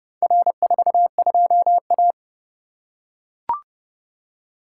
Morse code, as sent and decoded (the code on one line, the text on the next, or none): .-. ....- ..--- .-
R42A